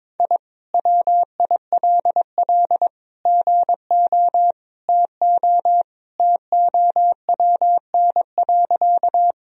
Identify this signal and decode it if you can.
Morse code, heard as a keyed tone, about 22 wpm, 700 Hz. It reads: I WILL GO TO TOWN.
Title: Sentences from Top 1000 Words - Review Part 2 - 22wpm